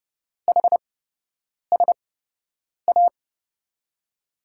Morse code, heard as a keyed tone, about 30 wpm, spaced as for 8 wpm, 700 Hz. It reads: HSA
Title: USA Call Signs - 8x Character Spacing - 30wpm